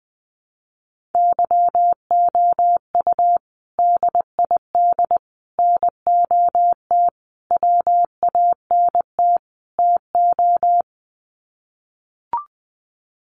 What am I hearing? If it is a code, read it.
Morse code, heard as a keyed tone, about 20 wpm, 700 Hz. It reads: YOU DID NOT WANT TO
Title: Sets of 5 Words Encore2 20wpm